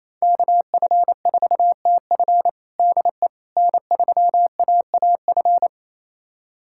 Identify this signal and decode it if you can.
Morse code, heard as a keyed tone, about 27 wpm, 700 Hz. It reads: KF4TF DE N3AAF
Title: USA Callsign DE Callsign - 5 Characters - 28wpm